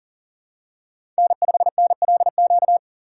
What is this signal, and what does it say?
Morse code, heard as a keyed tone, about 40 wpm, 700 Hz. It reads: N5NLQ
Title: USA Call Signs - 5 Characters - 2x Character Spacing - 40wpm